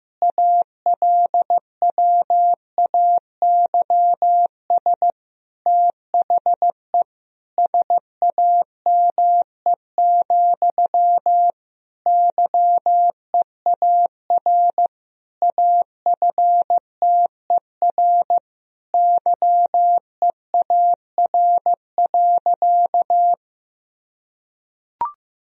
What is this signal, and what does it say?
Morse code, heard as a keyed tone, about 15 wpm, 700 Hz. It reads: ALWAYS THE SAME, YEAR AFTER YEAR.
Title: Sentences from Top 1000 Words - Review Part 1 - 15wpm